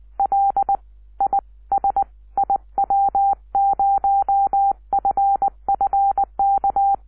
Morse code, heard as a keyed tone, about 20 wpm, 780 Hz. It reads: L I S IW0FFK
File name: iw0ffk / 2006 06 10 wq5w iw0ffk